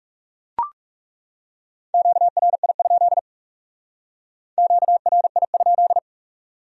Morse code, heard as a keyed tone, about 39 wpm, 700 Hz. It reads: QRI? QRI?